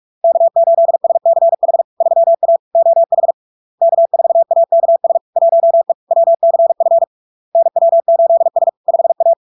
Morse code, heard as a keyed tone, about 46 wpm, 670 Hz. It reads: K8SCH 3A OH K4AKS 1E WCF NW8S 5A
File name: LARK-FD-2022-Log-in-CW-45wpm-670-Hz